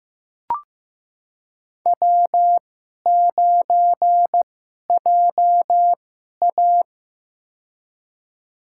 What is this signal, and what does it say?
Morse code, heard as a keyed tone, about 15 wpm, spaced as for 11 wpm, 700 Hz. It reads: W9JA
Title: USA Call Signs - 4 Characters - 2x Character Spacing - 15wpm